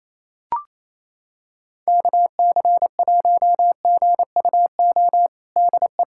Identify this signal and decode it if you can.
Morse code, heard as a keyed tone, about 28 wpm, 700 Hz. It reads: KC1GUO DE
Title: USA Callsign DE Callsign 28wpm